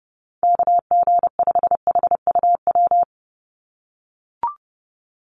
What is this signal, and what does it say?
Morse code, heard as a keyed tone, about 30 wpm, 700 Hz. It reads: KG5HUW
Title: USA Call Signs 30wpm